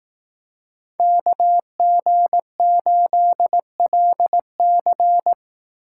Morse code, heard as a keyed tone, about 18 wpm, 700 Hz. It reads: KG8LC
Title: Speed Racing - USA 2x2 Call Signs 15wpm